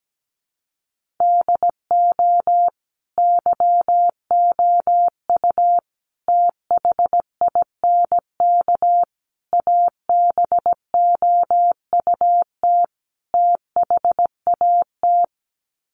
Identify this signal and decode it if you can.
Morse code, heard as a keyed tone, about 17 wpm, 700 Hz. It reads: DO YOU THINK ABOUT THAT
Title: Sets of 5 Words Encore2 17wpm